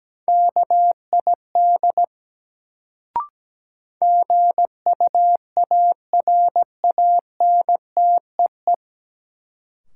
Morse code, heard as a keyed tone, about 17 wpm, 700 Hz. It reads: KID GUARANTEE
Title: Sentences from Top 1600 Words - New Word Drill Alt - 17wpm